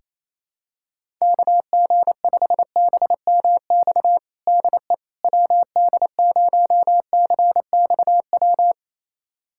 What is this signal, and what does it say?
Morse code, heard as a keyed tone, about 28 wpm, 700 Hz. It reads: KG5BMX DE WD0CXW
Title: USA Callsign DE Callsign - 6 Characters - 28wpm